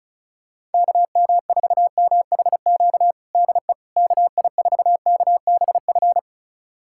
Morse code, heard as a keyed tone, about 34 wpm, 700 Hz. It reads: KM4MHQ DE KI4KBF
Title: USA Callsign DE Callsign - 6 Characters - 35wpm